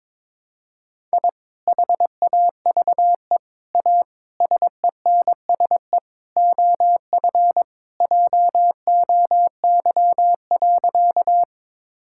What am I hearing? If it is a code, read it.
Morse code, heard as a keyed tone, about 22 wpm, 700 Hz. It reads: I HAVE A SENSE OF JOY.
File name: practice / 22wpm / Sentences with 3-Word Phrases - 22wpm